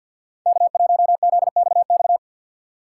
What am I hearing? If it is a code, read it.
Morse code, heard as a keyed tone, about 46 wpm, 700 Hz. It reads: K1ZXX
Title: USA Call Signs - 5 Characters - 2x Character Spacing - 50wpm